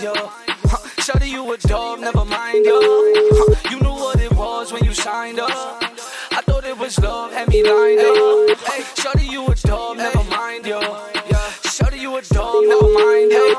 Ringback tone (Europe, 425 Hz), with noise.